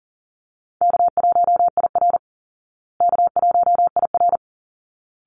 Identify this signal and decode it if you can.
Morse code, heard as a keyed tone, about 40 wpm, 700 Hz. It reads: K1IR K1IR
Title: Easy Call Signs Repeated 2x - 40wpm